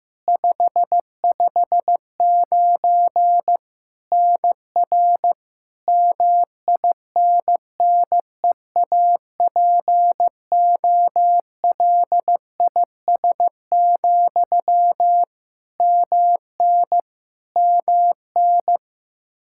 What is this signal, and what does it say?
Morse code, heard as a keyed tone, about 15 wpm, 700 Hz. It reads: 559 NR MINNEAPOLIS, MN MN
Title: Practice QSO - Exchange 1 - Beginning - 15wpm